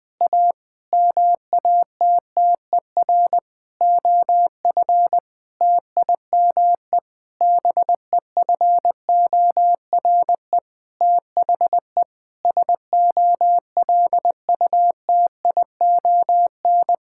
Morse code, heard as a keyed tone, about 20 wpm, 700 Hz. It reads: A MATTER OF TIME BEFORE THE SOLUTION
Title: Sentences with 9-Word Phrases - Spoken Beginning - 20wpm